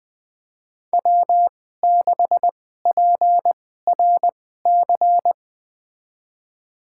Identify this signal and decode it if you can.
Morse code, heard as a keyed tone, about 20 wpm, spaced as for 15 wpm, 700 Hz. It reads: W6PRC
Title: USA Call Signs - 2x Character Spacing - 20wpm